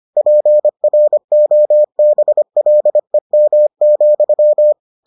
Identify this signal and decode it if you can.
Morse code, heard as a keyed tone, about 25 wpm, 600 Hz. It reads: PROBLEM,